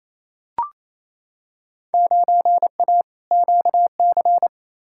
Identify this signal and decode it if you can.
Morse code, heard as a keyed tone, about 28 wpm, 700 Hz. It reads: 9A QC
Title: Field Day Exchanges Encore - 28wpm